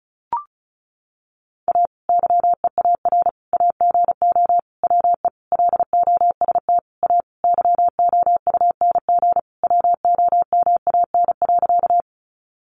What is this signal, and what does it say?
Morse code, heard as a keyed tone, about 34 wpm, 700 Hz. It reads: A YEAR AGO WE LOST A YOUNG WOMAN.
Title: Sentences from Top 1400 Words 35wpm